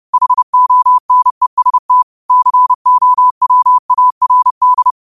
Morse code, heard as a keyed tone, about 30 wpm, 1000 Hz. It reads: HONEST COWARD
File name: qotdfiles / 30wpm